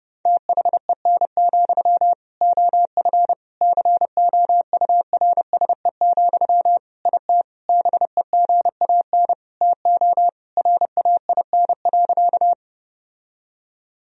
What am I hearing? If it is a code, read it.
Morse code, heard as a keyed tone, about 30 wpm, 700 Hz. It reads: THEN, OF COURSE, IT BEGAN TO RAIN.
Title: Sentences from Top 1000 Words - Review Part 1 - 30wpm